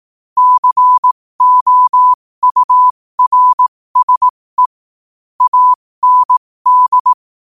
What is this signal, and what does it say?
Morse code, heard as a keyed tone, about 18 wpm, spaced as for 15 wpm, 1000 Hz. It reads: COURSE AND
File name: qotdfiles / 15wpm